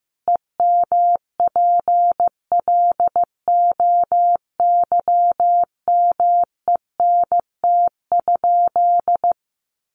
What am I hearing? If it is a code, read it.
Morse code, heard as a keyed tone, about 15 wpm, 700 Hz. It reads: EMPLOYMENT?
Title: Sentences from Top 1500 Words 15wpm